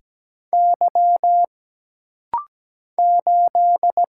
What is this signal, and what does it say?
Morse code, heard as a keyed tone, about 17 wpm, 700 Hz. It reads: Y 8
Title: Single Letter-Number - Rapid-Fire 17wpm